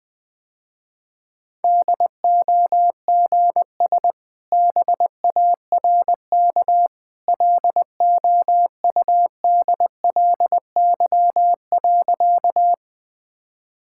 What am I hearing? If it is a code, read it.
Morse code, heard as a keyed tone, about 21 wpm, 700 Hz. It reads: DOGS BARK LOUDLY.
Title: Easy Sentences with Hints - 20wpm